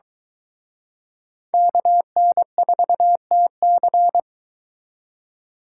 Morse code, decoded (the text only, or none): KN4TC